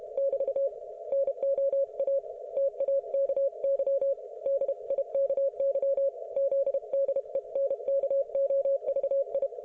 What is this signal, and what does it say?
Morse code, heard as a keyed tone, about 32 wpm, 550 Hz. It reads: <BT> NO A TAKY DIKY ZDENKOVI